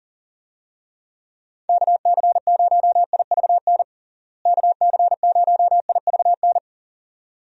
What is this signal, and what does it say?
Morse code, heard as a keyed tone, about 40 wpm, 700 Hz. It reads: KC0IVN KC0IVN
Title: USA Call Signs - 6 Characters - Repeated 2x - 40wpm